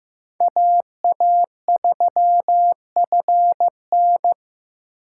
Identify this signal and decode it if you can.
Morse code, heard as a keyed tone, about 15 wpm, 700 Hz. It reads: AA3FN
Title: USA Call Signs - 5 Characters - 8x Character Spacing - 15wpm